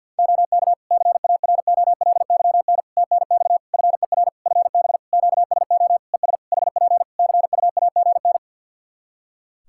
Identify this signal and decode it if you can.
Morse code, heard as a keyed tone, about 51 wpm, 700 Hz. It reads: OK CAROLYN TNX FER FB QSO ES HP CUAGN
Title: Practice QSO - Exchange 4 - Ending - 50wpm